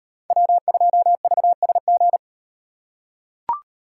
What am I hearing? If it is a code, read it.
Morse code, heard as a keyed tone, about 38 wpm, 700 Hz. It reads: W2VSG